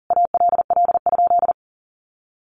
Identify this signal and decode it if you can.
Morse code, heard as a keyed tone, about 40 wpm, 700 Hz. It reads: ALL?
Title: Sentences from Top 1200 Words 40wpm